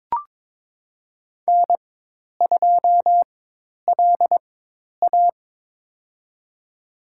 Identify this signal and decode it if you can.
Morse code, heard as a keyed tone, about 22 wpm, spaced as for 10 wpm, 700 Hz. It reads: N2LA
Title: USA Call Signs - 4x Character Spacing - 22wpm